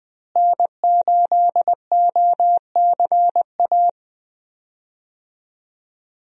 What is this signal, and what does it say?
Morse code, heard as a keyed tone, about 20 wpm, 700 Hz. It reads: N8OCA
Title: USA Call Signs - 5 Characters - Repeated 3x - 20wpm